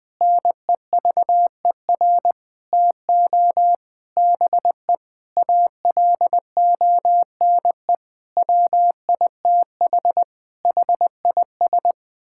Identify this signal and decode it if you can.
Morse code, heard as a keyed tone, about 20 wpm, 700 Hz. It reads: NEVER TO BE ALONE WITH HIS
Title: Sentences from Top 1000 Words - Review Part 5 - 20wpm